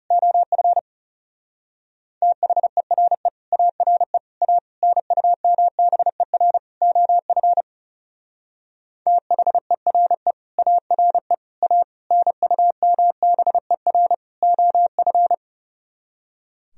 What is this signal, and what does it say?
Morse code, heard as a keyed tone, about 38 wpm, 700 Hz. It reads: OF THERE ARE A NUMBER OF THERE ARE A NUMBER OF